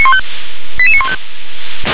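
Five-tone selective call (ZVEI), heard as quick runs of five tones, tones of about 70 ms. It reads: 88915
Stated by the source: ZVEI five-tone selective call 88786 received with an SDR.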